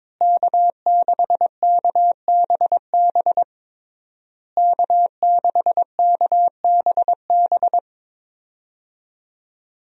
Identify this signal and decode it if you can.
Morse code, heard as a keyed tone, about 21 wpm, 700 Hz. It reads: K6KBB K6KBB
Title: USA Call Signs - 5 Characters - Repeated 3x - 22wpm